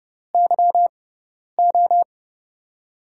Morse code, heard as a keyed tone, about 32 wpm, spaced as for 10 wpm, 700 Hz. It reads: YO
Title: USA Call Signs - 6 Characters - 6x Character Spacing - 30wpm